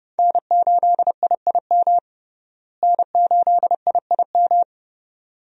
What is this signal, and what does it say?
Morse code, heard as a keyed tone, about 30 wpm, 700 Hz. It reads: N8IIM N8IIM
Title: USA Call Signs - 5 Characters - Repeated 2x - 30wpm